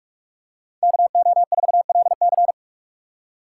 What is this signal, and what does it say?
Morse code, heard as a keyed tone, about 43 wpm, 700 Hz. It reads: KO4LC